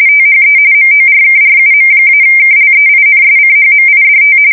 RTTY, about 45.45 baud. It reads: THE MORKRUM-KLEINSCHMIDT
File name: fsk170 / teletype-corp